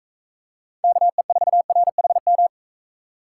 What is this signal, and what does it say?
Morse code, heard as a keyed tone, about 40 wpm, 700 Hz. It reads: KE4RHM